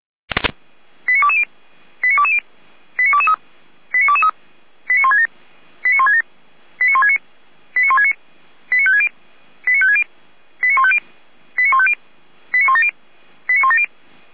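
Five-tone selective call (ZVEI), heard as quick runs of five tones, tones of about 70 ms. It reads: 89220 89220 89223 89223 89167 89167 89169 89169 89560 89560 89150 89150 89170 89170
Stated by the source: ZVEI five-tone selective call received on an SDR.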